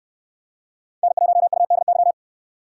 Morse code, heard as a keyed tone, about 64 wpm, 700 Hz. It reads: N1UDY